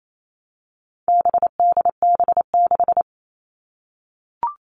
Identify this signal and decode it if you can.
Morse code, heard as a keyed tone, about 28 wpm, 700 Hz. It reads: BDB6
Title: All Cut Numbers - 4-Digit 28wpm